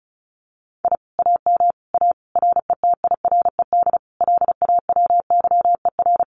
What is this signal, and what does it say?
Morse code, heard as a keyed tone, about 34 wpm, 700 Hz. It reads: I AM A RETIRED LAWYER
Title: Common QSO Phrases Encore 35wpm